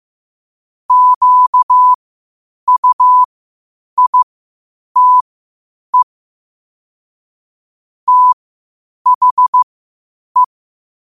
Morse code, heard as a keyed tone, about 15 wpm, spaced as for 8 wpm, 1000 Hz. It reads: QUITE THE
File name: qotdfiles / 8wpm